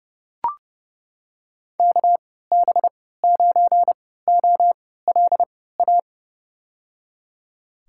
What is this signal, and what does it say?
Morse code, heard as a keyed tone, about 30 wpm, spaced as for 17 wpm, 700 Hz. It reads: KB9OLA
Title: USA Call Signs - 6 Characters - 3x Character Spacing - 30wpm